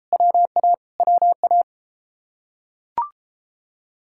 Morse code, heard as a keyed tone, about 33 wpm, 700 Hz. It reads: WA WA